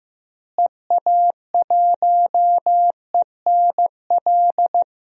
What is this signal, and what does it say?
Morse code, heard as a keyed tone, about 14 wpm, 700 Hz. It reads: EA1ENL
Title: DXCC-100 Call Signs - 6 Characters - Repeated 3x - 15wpm